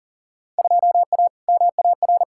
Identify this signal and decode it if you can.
Morse code, heard as a keyed tone, about 40 wpm, 700 Hz. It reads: 2A MAR
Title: Field Day Exchanges - 40wpm